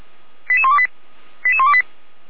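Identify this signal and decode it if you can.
Five-tone selective call (ZVEI), heard as quick runs of five tones, tones of about 70 ms. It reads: 88128 88128
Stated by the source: Five-tone ZVEI selective call received on an SDR.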